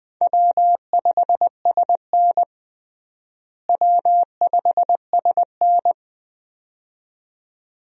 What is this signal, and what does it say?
Morse code, heard as a keyed tone, about 20 wpm, 700 Hz. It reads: W5SN W5SN
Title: USA Call Signs - 4 Characters - Repeated 2x - 20wpm